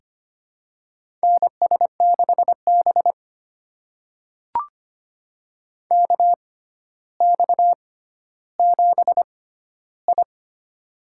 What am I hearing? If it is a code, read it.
Morse code, heard as a keyed tone, about 25 wpm, spaced as for 13 wpm, 700 Hz. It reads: NS6B KX7I
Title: USA Call Signs - 4 Characters - 6x Character Spacing - 25wpm